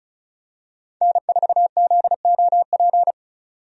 Morse code, heard as a keyed tone, about 34 wpm, 700 Hz. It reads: N4ZOP
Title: USA Call Signs - 5 Characters - 35wpm